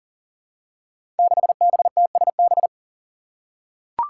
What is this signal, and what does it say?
Morse code, heard as a keyed tone, about 42 wpm, 700 Hz. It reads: 6BTSB
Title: Five Challenging Letter-Number Combinations 40wpm